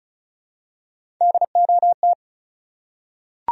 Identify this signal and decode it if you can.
Morse code, heard as a keyed tone, about 36 wpm, 700 Hz. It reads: DOT
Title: USA Common Names - Max 4 Letters - 35wpm